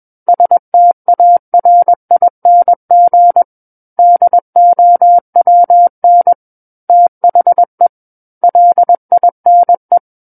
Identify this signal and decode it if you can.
Morse code, heard as a keyed tone, about 21 wpm, 700 Hz. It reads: STARING DOWN THE LINE
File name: Book One / 12-Chapter 12 - 21 WPM